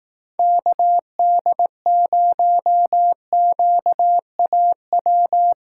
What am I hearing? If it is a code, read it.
Morse code, heard as a keyed tone, about 18 wpm, 700 Hz. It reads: KD0QAW
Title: Speed Racing - USA Call Signs 15wpm